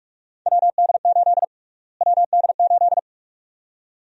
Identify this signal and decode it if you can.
Morse code, heard as a keyed tone, about 46 wpm, 700 Hz. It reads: WD8 WD8
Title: USA Call Sign Prefixes - Repeated 2x - 45wpm